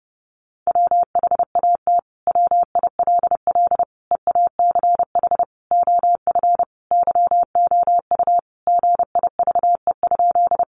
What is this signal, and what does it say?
Morse code, heard as a keyed tone, about 30 wpm, 700 Hz. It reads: WHAT WILL EACH OF YOU GIVE?
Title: Sentences from Top 200 Words 30wpm